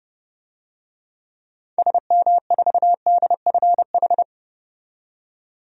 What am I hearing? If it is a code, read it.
Morse code, heard as a keyed tone, about 30 wpm, 700 Hz. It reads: SM4DFH
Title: DXCC-100 Call Signs - 6 Characters - 30wpm